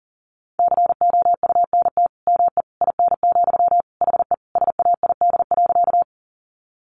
Morse code, heard as a keyed tone, about 40 wpm, 700 Hz. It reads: COUNT ME IN, HE SAID.
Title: Sentences from Top 1500 Words 40wpm